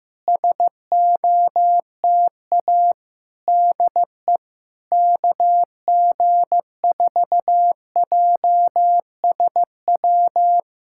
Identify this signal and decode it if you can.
Morse code, heard as a keyed tone, about 15 wpm, 700 Hz. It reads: SOTA DE KG4JSW
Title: SOTA QSO Practice 15wpm